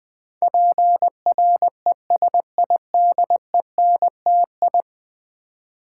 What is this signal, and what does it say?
Morse code, heard as a keyed tone, about 20 wpm, 700 Hz. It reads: PRESIDENTI
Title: Build A Word - 5 Letters At A Time - 12-Letter Words - 20wpm